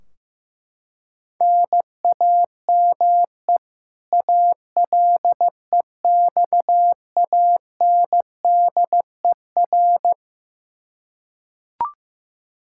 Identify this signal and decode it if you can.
Morse code, heard as a keyed tone, about 15 wpm, 700 Hz. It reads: NAME ALEXANDER
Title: Practice QSO - Exchange 1 - Beginning - 15wpm